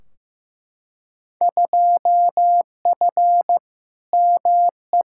Morse code, heard as a keyed tone, about 15 wpm, 700 Hz. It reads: 2F ME
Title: Field Day Exchanges - 15wpm